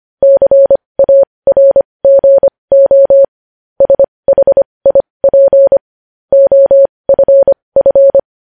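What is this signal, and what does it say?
Morse code, heard as a keyed tone, about 25 wpm, spaced as for 20 wpm, 550 Hz. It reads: CARGO SHIP OFF
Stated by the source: Morse news headlines sent at 20.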